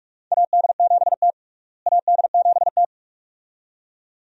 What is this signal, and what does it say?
Morse code, heard as a keyed tone, about 44 wpm, 700 Hz. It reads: AD7T AD7T
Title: USA Call Signs - 4 Characters - Repeated 2x - 45wpm